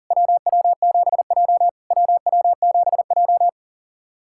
Morse code, heard as a keyed tone, about 40 wpm, 700 Hz. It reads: WW7J WW7J
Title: SOTA QSO Practice 40wpm